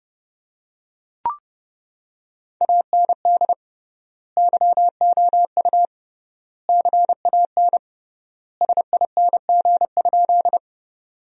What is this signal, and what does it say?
Morse code, heard as a keyed tone, about 30 wpm, 700 Hz. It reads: AND YOU CAN SING?
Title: 3x Word-Spacing - Sentences from Top 400 Words - 30wpm